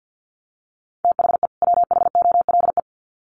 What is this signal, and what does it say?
Morse code, heard as a keyed tone, about 48 wpm, 700 Hz. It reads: THE WHOLE
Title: Sets of 2 Words 50wpm